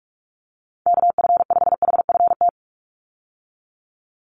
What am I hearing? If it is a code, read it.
Morse code, heard as a keyed tone, about 44 wpm, 700 Hz. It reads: KF5HFT
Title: USA Call Signs 45wpm